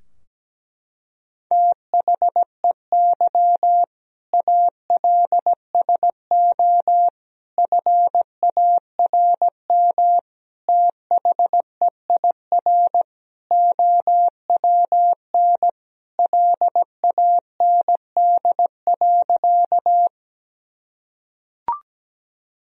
Morse code, heard as a keyed tone, about 17 wpm, 700 Hz. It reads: THEY ALSO FARM THEIR OWN LAND.
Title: Sentences from Top 1000 Words - Review Part 2 - 17wpm